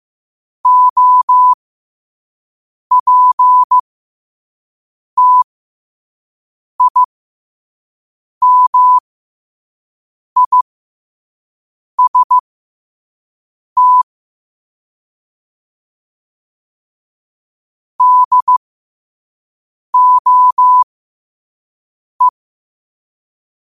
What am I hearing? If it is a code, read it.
Morse code, heard as a keyed tone, about 15 wpm, spaced as for 5 wpm, 1000 Hz. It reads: OPTIMIST DOE
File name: qotdfiles / 5wpm